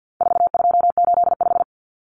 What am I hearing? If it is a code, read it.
Morse code, heard as a keyed tone, about 50 wpm, 700 Hz. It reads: 4285
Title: Common Cut Numbers - 4-Digit 50wpm